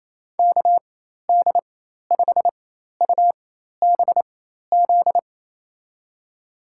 Morse code, heard as a keyed tone, about 28 wpm, spaced as for 13 wpm, 700 Hz. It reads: KD5UBZ
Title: USA Call Signs - 6 Characters - 4x Character Spacing - 28wpm